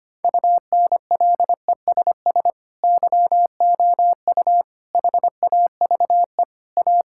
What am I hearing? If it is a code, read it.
Morse code, heard as a keyed tone, about 25 wpm, 700 Hz. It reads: UNLESS YOU HAVE A
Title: Sentences from Top 1000 Words - Review Part 5 - 25wpm